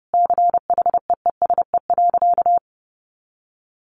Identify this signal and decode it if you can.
Morse code, heard as a keyed tone, about 30 wpm, 700 Hz. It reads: CHEESE.
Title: Easy Phrases 30wpm